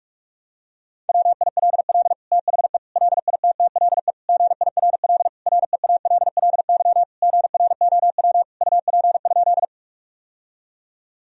Morse code, heard as a keyed tone, about 46 wpm, 700 Hz. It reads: WILL THE LITTLE GIRL REALLY GROW UP?
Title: Sentences from Top 1000 Words - Review Part 1 - 45wpm